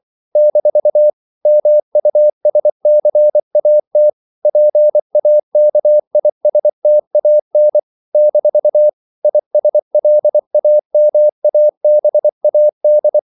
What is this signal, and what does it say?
Morse code, heard as a keyed tone, about 24 wpm, 600 Hz. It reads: - MUSCAT PAKISTAN - ISLAMABAD
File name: CW / Practice / NationalCapitals24